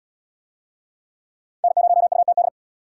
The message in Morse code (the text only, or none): N0UTD